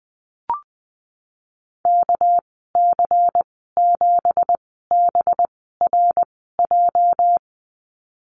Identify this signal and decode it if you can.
Morse code, heard as a keyed tone, about 20 wpm, spaced as for 15 wpm, 700 Hz. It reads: KC7BRJ
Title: 219 Lesson 35g - Number 0 - Call Signs 20wpm